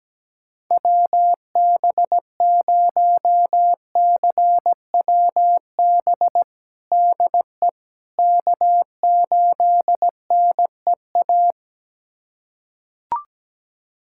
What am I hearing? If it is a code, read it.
Morse code, heard as a keyed tone, about 17 wpm, 700 Hz. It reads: WB0CWB DE K8NEA
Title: USA Callsign DE Callsign 17wpm